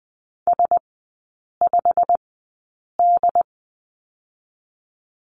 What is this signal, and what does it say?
Morse code, heard as a keyed tone, about 20 wpm, spaced as for 8 wpm, 700 Hz. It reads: S5D
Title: 087 Lesson 15e - Number 5 - Three Characters 20wpm